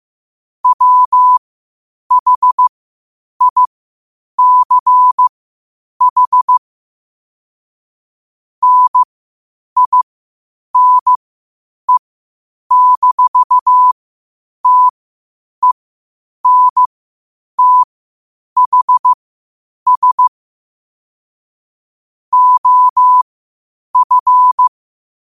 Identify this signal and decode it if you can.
Morse code, heard as a keyed tone, about 15 wpm, spaced as for 8 wpm, 1000 Hz. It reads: WHICH NINE-TENTHS OF
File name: qotdfiles / 8wpm